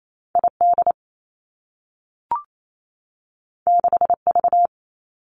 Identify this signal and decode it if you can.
Morse code, heard as a keyed tone, about 28 wpm, 700 Hz. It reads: I D 6 V
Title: Two Challenging Letter-Number Combinations 28wpm